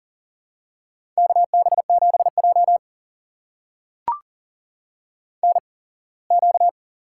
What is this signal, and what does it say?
Morse code, heard as a keyed tone, about 40 wpm, spaced as for 24 wpm, 700 Hz. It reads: KB7J NQ
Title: USA Call Signs - 4 Characters - 8x Character Spacing - 40wpm